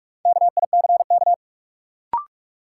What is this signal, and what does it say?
Morse code, heard as a keyed tone, about 41 wpm, 700 Hz. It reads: KICK
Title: Sentences from Top 1300 Words - New Word Drill - 45wpm